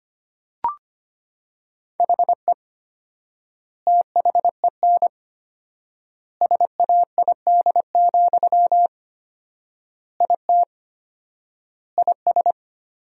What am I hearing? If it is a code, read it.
Morse code, heard as a keyed tone, about 25 wpm, 700 Hz. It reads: HE THEN SAID, IT IS
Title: 4x Word-Spacing - Sentences from Top 300 Words - 25wpm